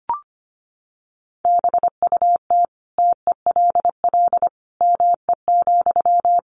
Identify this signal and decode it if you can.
Morse code, heard as a keyed tone, about 25 wpm, 700 Hz. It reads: BUT TELL ME,
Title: Sentences from Top 1500 Words 25wpm